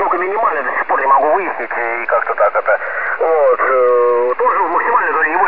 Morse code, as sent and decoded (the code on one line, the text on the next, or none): none